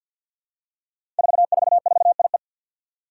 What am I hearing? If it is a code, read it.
Morse code, heard as a keyed tone, about 53 wpm, 700 Hz. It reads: 444IE
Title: Five Challenging Letter-Number Combinations 50wpm